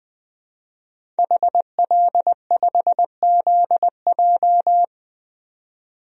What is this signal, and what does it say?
Morse code, heard as a keyed tone, about 20 wpm, 700 Hz. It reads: HL5ZJ
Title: DXCC-100 Call Signs - 5 Characters - 20wpm